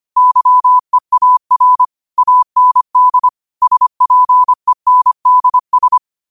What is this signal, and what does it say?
Morse code, heard as a keyed tone, about 25 wpm, 1000 Hz. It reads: YEAR AND SPENDS
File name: qotdfiles / 25wpm